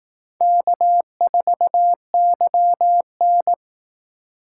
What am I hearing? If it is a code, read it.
Morse code, heard as a keyed tone, about 19 wpm, 700 Hz. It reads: K4YN
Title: Speed Racing - USA Call Signs 15wpm